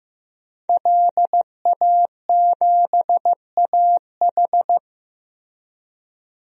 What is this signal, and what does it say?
Morse code, heard as a keyed tone, about 15 wpm, 700 Hz. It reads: LA7AH
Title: DXCC-100 Call Signs - 5 Characters - Repeated 2x - 15wpm